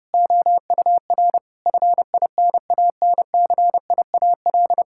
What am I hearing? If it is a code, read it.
Morse code, heard as a keyed tone, about 32 wpm, 700 Hz. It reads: OUR FINANCIAL
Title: Sentences from Top 1000 Words - Review Part 5 - 30wpm